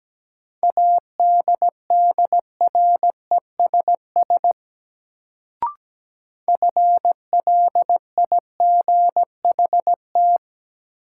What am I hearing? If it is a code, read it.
Morse code, heard as a keyed tone, about 17 wpm, 700 Hz. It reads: ADDRESS FLIGHT
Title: Sentences from Top 1000 Words - New Word Drill - 17wpm